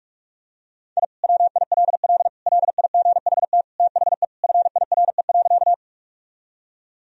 Morse code, heard as a keyed tone, about 41 wpm, 700 Hz. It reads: I WILL LIGHT THE FIRE.
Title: Sentences from Top 1000 Words - Review Part 1 - 45wpm